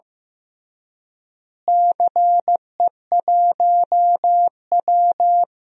Morse code, heard as a keyed tone, about 15 wpm, 700 Hz. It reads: CE1W
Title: DXCC-100 Call Signs - 4 Characters - Repeated 2x - 15wpm